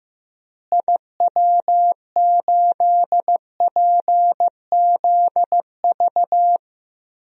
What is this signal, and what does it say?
Morse code, heard as a keyed tone, about 15 wpm, 700 Hz. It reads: IW8PZV
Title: DXCC-100 Call Signs - 6 Characters - Repeated 3x - 15wpm